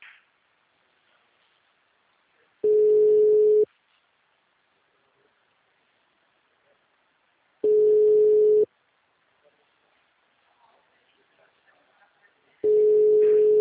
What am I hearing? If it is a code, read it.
Ringback tone (Europe, 425 Hz).